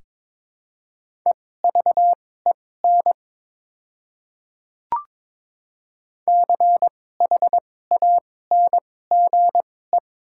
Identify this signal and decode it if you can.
Morse code, heard as a keyed tone, about 22 wpm, spaced as for 16 wpm, 690 Hz. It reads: EVEN CHANGE
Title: 2x Character Spacing - Top 500 Words - 22wpm